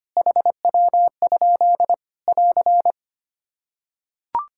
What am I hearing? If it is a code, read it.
Morse code, heard as a keyed tone, about 25 wpm, 700 Hz. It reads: HW? <AR>
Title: K1USN SST QSO Practice 25wpm